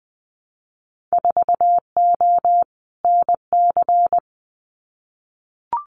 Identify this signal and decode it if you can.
Morse code, heard as a keyed tone, about 20 wpm, 700 Hz. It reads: 4O NC
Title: Winter Field Day Exchanges 20wpm